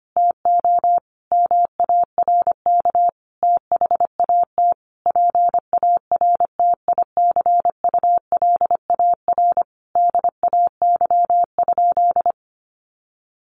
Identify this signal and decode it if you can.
Morse code, heard as a keyed tone, about 25 wpm, 700 Hz. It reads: TO MARK THAT PARTICULAR DAY?
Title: Sentences from Top 1400 Words 25wpm